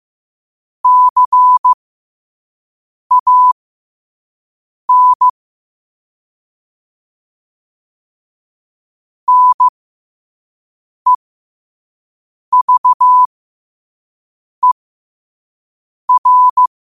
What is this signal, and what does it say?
Morse code, heard as a keyed tone, about 15 wpm, spaced as for 5 wpm, 1000 Hz. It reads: CAN NEVER
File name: qotdfiles / 5wpm